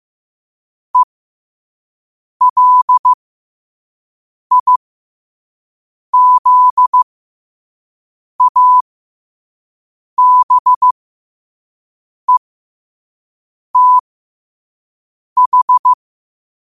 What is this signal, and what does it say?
Morse code, heard as a keyed tone, about 15 wpm, spaced as for 5 wpm, 1000 Hz. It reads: ELIZABETH
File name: qotdfiles / 5wpm